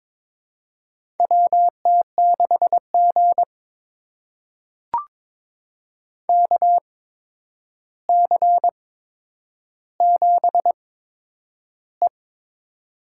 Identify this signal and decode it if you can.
Morse code, heard as a keyed tone, about 22 wpm, spaced as for 9 wpm, 700 Hz. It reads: WT6G KC7E
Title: USA Call Signs - 4 Characters - 8x Character Spacing - 22wpm